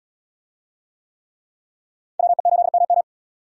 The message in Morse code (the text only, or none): WE8MM